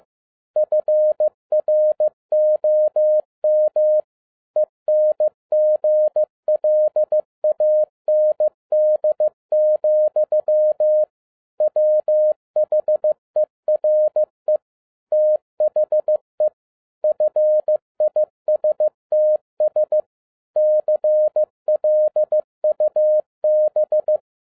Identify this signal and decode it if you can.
Morse code, heard as a keyed tone, about 15 wpm, 610 Hz. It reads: FROM ENGLAND, WHERE THE FISTS CLUB
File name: 2011-10-mp3 / n12-JL3YMV 15wpm